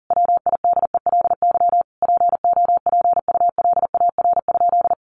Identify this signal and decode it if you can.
Morse code, heard as a keyed tone, about 40 wpm, 700 Hz. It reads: WIDELY POPULAR?
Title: Sentences from Top 1200 Words 40wpm